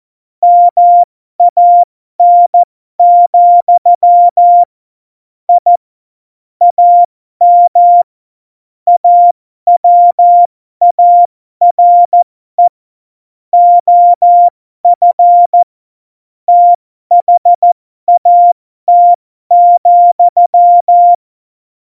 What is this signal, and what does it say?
Morse code, heard as a keyed tone, about 14 wpm, spaced as for 12 wpm, 700 Hz. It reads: MAN, I AM AWARE OF THAT,
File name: Morsekurs / Witze